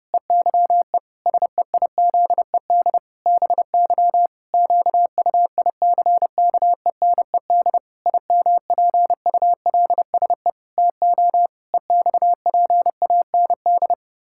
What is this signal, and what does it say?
Morse code, heard as a keyed tone, about 30 wpm, 700 Hz. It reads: EYE SEIZED BY QUICKENED IMPULSE TO EXPAND
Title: Pangram Challenge v2 - 30wpm